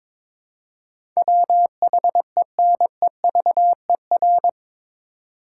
Morse code, heard as a keyed tone, about 22 wpm, 700 Hz. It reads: WHENEVER
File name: practice / 22wpm / Eight Letter Words - 22wpm